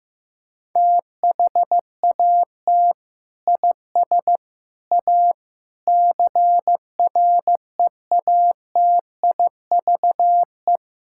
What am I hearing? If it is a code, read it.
Morse code, heard as a keyed tone, about 15 wpm, 700 Hz. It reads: THAT IS A CREATIVE
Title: Sentences from Top 1000 Words - Review Part 5 - 15wpm